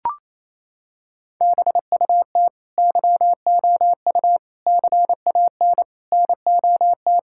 Morse code, heard as a keyed tone, about 28 wpm, 700 Hz. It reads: BUT YOU CAN NOT
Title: Sentences from Top 1000 Words - Review Part 6 - 28wpm